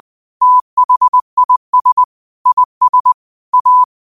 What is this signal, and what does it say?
Morse code, heard as a keyed tone, about 20 wpm, 1000 Hz. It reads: THIS IS A
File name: qotdfiles / 20wpm